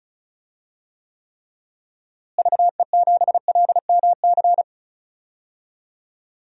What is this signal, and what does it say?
Morse code, heard as a keyed tone, about 34 wpm, 700 Hz. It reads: VE7LMC
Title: DXCC-100 Call Signs - 6 Characters - Repeated 3x - 35wpm